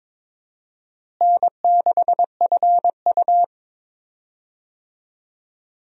Morse code, heard as a keyed tone, about 22 wpm, 700 Hz. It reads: N6FU
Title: USA Call Signs - 4 Characters - Repeated 3x - 22wpm